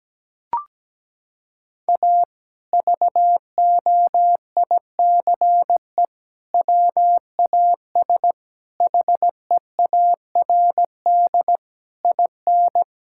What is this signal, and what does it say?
Morse code, heard as a keyed tone, about 17 wpm, 700 Hz. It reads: A VOICE WAS HEARD IN
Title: Sentences from Top 1000 Words - Review Part 5 - 17wpm